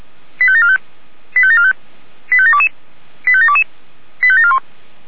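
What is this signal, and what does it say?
Five-tone selective call (ZVEI), heard as quick runs of five tones, tones of about 70 ms. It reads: 85745 85745 85722 85722 85741
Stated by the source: ZVEI five-tone call received by SDR.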